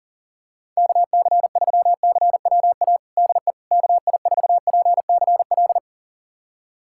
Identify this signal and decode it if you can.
Morse code, heard as a keyed tone, about 40 wpm, 700 Hz. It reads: KC3CWA DE KI4PCL
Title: USA Callsign DE Callsign 40wpm